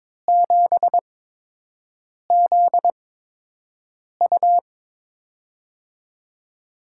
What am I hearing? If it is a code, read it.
Morse code, heard as a keyed tone, about 21 wpm, spaced as for 6 wpm, 700 Hz. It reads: 7ZU
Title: USA Call Signs - 5 Characters - 8x Character Spacing - 22wpm